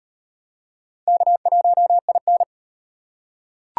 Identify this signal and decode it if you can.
Morse code, heard as a keyed tone, about 38 wpm, 700 Hz. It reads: K1IN